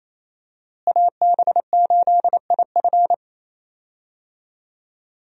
Morse code, heard as a keyed tone, about 27 wpm, 700 Hz. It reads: AB8IF
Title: USA Call Signs - 5 Characters - Repeated 2x - 28wpm